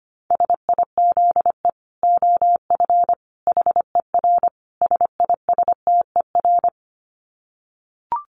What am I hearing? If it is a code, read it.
Morse code, heard as a keyed tone, about 25 wpm, 700 Hz. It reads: SIZE OF HER SISTER
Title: Dit Challenge 25wpm